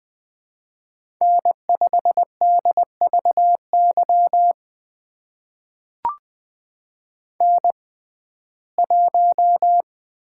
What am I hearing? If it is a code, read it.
Morse code, heard as a keyed tone, about 20 wpm, spaced as for 14 wpm, 700 Hz. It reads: N5DVY N1
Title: USA Call Signs - 5 Characters - 6x Character Spacing - 20wpm